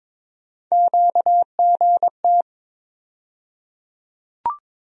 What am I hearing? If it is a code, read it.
Morse code, heard as a keyed tone, about 22 wpm, 700 Hz. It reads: QGT
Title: USA Call Sign Postfixes - 3 Letters 22wpm